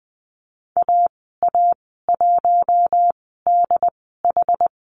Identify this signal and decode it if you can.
Morse code, heard as a keyed tone, about 20 wpm, spaced as for 15 wpm, 700 Hz. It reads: AA1DH
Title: 161 Lesson 26g - Letter g - Call Signs 20wpm